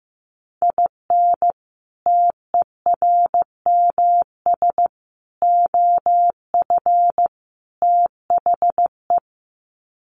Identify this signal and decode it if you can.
Morse code, heard as a keyed tone, about 14 wpm, 700 Hz. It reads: IN TERMS OF THE
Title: Sets of 4 Words 15wpm